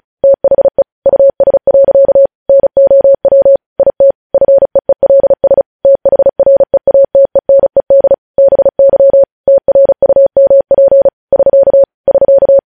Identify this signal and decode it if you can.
Morse code, heard as a keyed tone, about 36 wpm, 550 Hz. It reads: THE US. NOW IT FEELS THREATENED BY TRUMP <SK> <SK>
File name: podcast / News headlines-1x-35x35